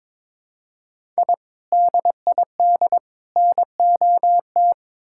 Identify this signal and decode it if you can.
Morse code, heard as a keyed tone, about 22 wpm, 700 Hz. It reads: I DID NOT
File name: practice / 22wpm / Build A Phrase - 5-Word Phrases - 22wpm